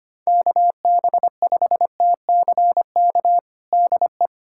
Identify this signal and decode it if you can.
Morse code, heard as a keyed tone, about 25 wpm, 700 Hz. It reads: KB5TCK DE
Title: USA Callsign DE Callsign - 6 Characters - 25wpm